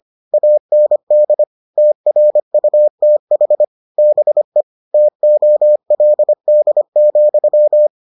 Morse code, heard as a keyed tone, about 25 wpm, 600 Hz. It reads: AND TRUTH BE TOLD,